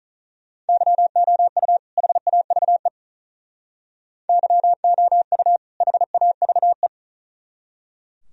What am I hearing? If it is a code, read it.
Morse code, heard as a keyed tone, about 44 wpm, 700 Hz. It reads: YOU HAVE YOU HAVE
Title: Speed-Racing - Sets of 2 Words 35wpm